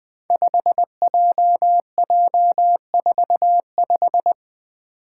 Morse code, heard as a keyed tone, about 20 wpm, 700 Hz. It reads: 5JJ45
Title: Five Challenging Letter-Number Combinations 20wpm